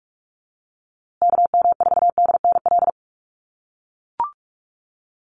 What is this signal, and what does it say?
Morse code, heard as a keyed tone, about 44 wpm, 700 Hz. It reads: KM4DNL
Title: USA Call Signs Repeated 3x 45wpm